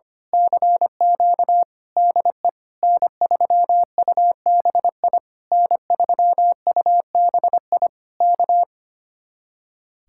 Morse code, heard as a keyed tone, about 24 wpm, 700 Hz. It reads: CQ DE N3UBI N3UBI K
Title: Speed-Racing Common QSO Phrases Encore - 25wpm